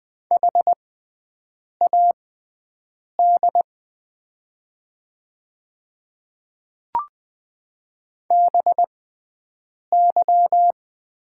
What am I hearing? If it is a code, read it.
Morse code, heard as a keyed tone, about 20 wpm, spaced as for 7 wpm, 700 Hz. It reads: HAD BY
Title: 6x Character Spacing - Top 200 Words - 20wpm